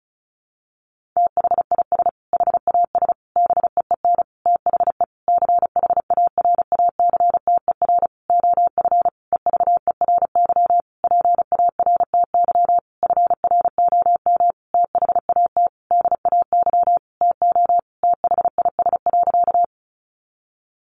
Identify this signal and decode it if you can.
Morse code, heard as a keyed tone, about 34 wpm, 700 Hz. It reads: THIS HAS BEEN THE CHARACTER OF EVERY PARTY FROM THAT DAY TO THIS.
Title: Sentences from Top 1400 Words 35wpm